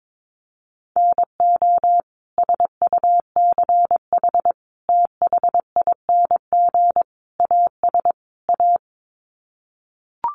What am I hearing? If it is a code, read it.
Morse code, heard as a keyed tone, about 22 wpm, 700 Hz. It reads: NO SUCH THING AS A
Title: Sets of 5 Words Encore 22wpm